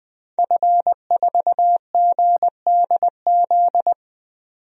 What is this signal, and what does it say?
Morse code, heard as a keyed tone, about 20 wpm, 700 Hz. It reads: F4GDZ
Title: DXCC-100 Call Signs - 5 Characters - 20wpm